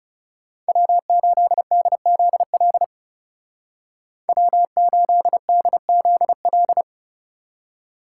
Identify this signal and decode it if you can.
Morse code, heard as a keyed tone, about 36 wpm, 700 Hz. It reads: W8DZL W8DZL
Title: Speed Racing - USA 1x3 Call Signs 30wpm